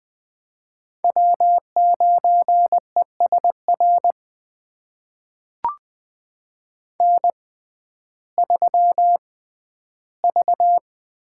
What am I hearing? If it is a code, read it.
Morse code, heard as a keyed tone, about 20 wpm, spaced as for 12 wpm, 700 Hz. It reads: W9ESR N3V
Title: USA Call Signs - 5 Characters - 6x Character Spacing - 20wpm